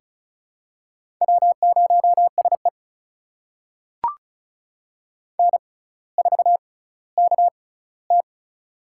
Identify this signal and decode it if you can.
Morse code, heard as a keyed tone, about 34 wpm, spaced as for 18 wpm, 700 Hz. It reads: W0SE N4KT
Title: USA Call Signs - 4 Characters - 6x Character Spacing - 35wpm